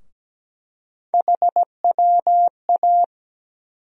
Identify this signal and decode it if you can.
Morse code, heard as a keyed tone, about 17 wpm, 700 Hz. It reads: HWA
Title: USA Call Sign Postfixes - 3 Letters 17wpm